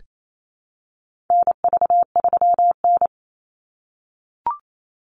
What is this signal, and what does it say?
Morse code, heard as a keyed tone, about 28 wpm, 700 Hz. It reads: NV3N
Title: Call Signs - 28wpm